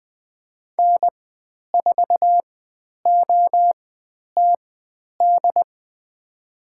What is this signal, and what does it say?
Morse code, heard as a keyed tone, about 20 wpm, spaced as for 10 wpm, 700 Hz. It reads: N4OTD
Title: 109 Lesson 18g - Letter m - Call Signs 20wpm